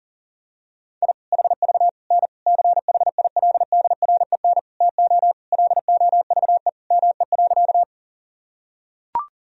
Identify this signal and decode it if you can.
Morse code, heard as a keyed tone, about 40 wpm, 700 Hz. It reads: I HV N CHILDREN TO LOVE ME.
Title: Sentences from Top 400 Words and Abbreviations 40wpm